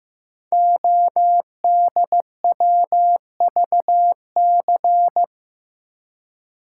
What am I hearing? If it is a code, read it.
Morse code, heard as a keyed tone, about 15 wpm, 690 Hz. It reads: ODWVC
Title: Code Groups - 5 Letters - 15wpm